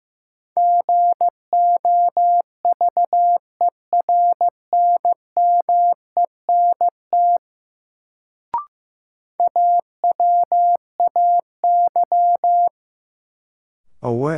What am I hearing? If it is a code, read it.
Morse code, heard as a keyed tone, about 15 wpm, 700 Hz. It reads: GOVERNMENT AWAY
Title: Sentences from Top 600 Words - New Word Drill - 15wpm